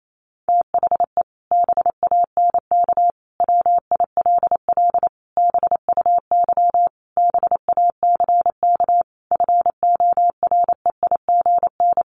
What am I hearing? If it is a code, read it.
Morse code, heard as a keyed tone, about 28 wpm, 700 Hz. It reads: THE BANK WILL BUY BACK FOREIGN
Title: Sentences from Top 1500 Words 28wpm